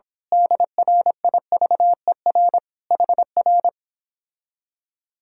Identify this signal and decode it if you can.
Morse code, heard as a keyed tone, about 26 wpm, 700 Hz. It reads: DRIVER HR
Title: Speed-Racing Common QSO Phrases - 22wpm